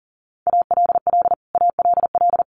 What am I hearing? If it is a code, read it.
Morse code, heard as a keyed tone, about 37 wpm, 700 Hz. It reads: ALL ALL
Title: Top 200 Words Repeated 40wpm